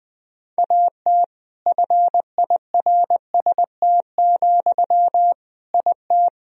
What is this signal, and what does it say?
Morse code, heard as a keyed tone, about 20 wpm, 700 Hz. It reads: AT FIRST, IT
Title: Sentences from Top 1000 Words - Review Part 4 - 20wpm